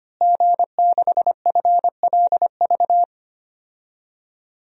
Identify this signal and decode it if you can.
Morse code, heard as a keyed tone, about 25 wpm, 700 Hz. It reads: G6FLV
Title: DXCC-100 Call Signs - 5 Characters - Repeated 3x - 25wpm